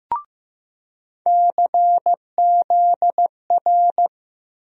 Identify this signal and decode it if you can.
Morse code, heard as a keyed tone, about 15 wpm, 700 Hz. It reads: CZR
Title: USA Call Sign Postfixes - 3 Letters 15wpm